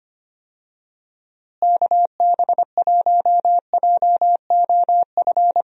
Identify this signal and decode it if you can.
Morse code, heard as a keyed tone, about 25 wpm, 700 Hz. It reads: KB1JOF
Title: USA Call Signs - 6 Characters - Repeated 2x - 25wpm